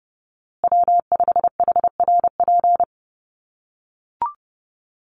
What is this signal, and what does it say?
Morse code, heard as a keyed tone, about 30 wpm, 700 Hz. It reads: W5HRP
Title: USA Call Signs Repeated 2x 30wpm